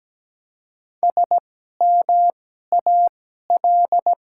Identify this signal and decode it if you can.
Morse code, heard as a keyed tone, about 17 wpm, spaced as for 12 wpm, 700 Hz. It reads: SMAL
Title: 2x Character Spacing - Top 200 Words - 17wpm